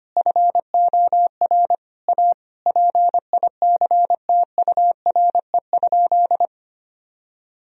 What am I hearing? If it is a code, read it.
Morse code, heard as a keyed tone, about 25 wpm, 700 Hz. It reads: FOR A PICTURE?
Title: Sentences from Top 1000 Words - Review Part 2 - 25wpm